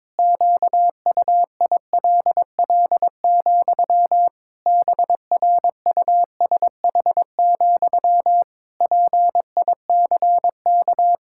Morse code, heard as a keyed tone, about 22 wpm, 700 Hz. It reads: QUILL, BRUSH, PICK
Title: Pangram Challenge v2 - 22wpm